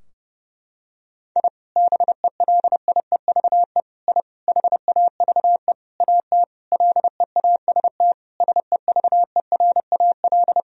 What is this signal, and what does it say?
Morse code, heard as a keyed tone, about 32 wpm, 700 Hz. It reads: I BELIEVE I HAVE AT LEAST SEVERAL
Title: Sentences from Top 1000 Words - Review Part 2 - 30wpm